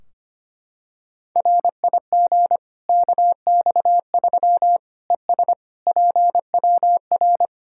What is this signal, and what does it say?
Morse code, heard as a keyed tone, about 25 wpm, 700 Hz. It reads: RIG KX3 ES PWR
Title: Practice QSO - Exchange 2 - Info - 25wpm